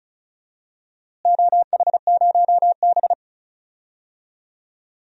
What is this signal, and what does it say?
Morse code, heard as a keyed tone, about 36 wpm, 700 Hz. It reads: OH0B
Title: DXCC-100 Call Signs - 4 Characters - Repeated 3x - 35wpm